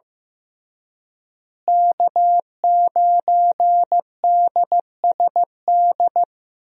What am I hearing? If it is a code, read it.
Morse code, heard as a keyed tone, about 15 wpm, 700 Hz. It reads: K9DSD
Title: USA Call Signs - 5 Characters - Repeated 3x - 15wpm